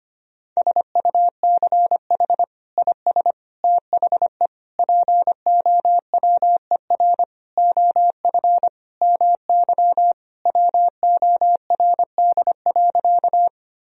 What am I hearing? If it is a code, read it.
Morse code, heard as a keyed tone, about 25 wpm, 700 Hz. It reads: SUCH IS THE POWER OF MY WORD.
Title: Sentences from Top 1000 Words - Review Part 2 - 25wpm